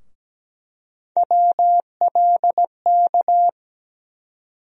Morse code, heard as a keyed tone, about 17 wpm, 700 Hz. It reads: WLK
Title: USA Call Sign Postfixes - 3 Letters 17wpm